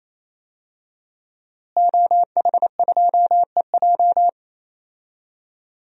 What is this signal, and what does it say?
Morse code, heard as a keyed tone, about 28 wpm, 700 Hz. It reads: OH2EJ
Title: DXCC-100 Call Signs - 5 Characters - Repeated 3x - 28wpm